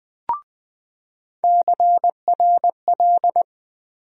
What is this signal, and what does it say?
Morse code, heard as a keyed tone, about 20 wpm, 700 Hz. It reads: CRL
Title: USA Call Sign Postfixes - 3 Letters 20wpm